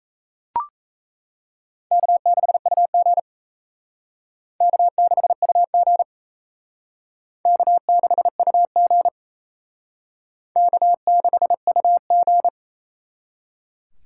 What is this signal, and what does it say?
Morse code, heard as a keyed tone, about 36 wpm, 700 Hz. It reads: K6UG K6UG K6UG K6UG